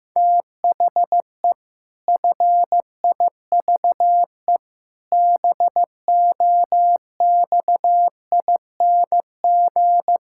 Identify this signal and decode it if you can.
Morse code, heard as a keyed tone, about 15 wpm, 700 Hz. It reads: THE FIVE BOXING
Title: Pangram Challenge v2 - 15wpm